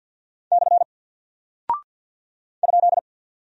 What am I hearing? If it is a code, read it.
Morse code, heard as a keyed tone, about 48 wpm, 700 Hz. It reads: / ?
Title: Single Character (Letter:Number:Punctuation:Prosign) Alt 50wpm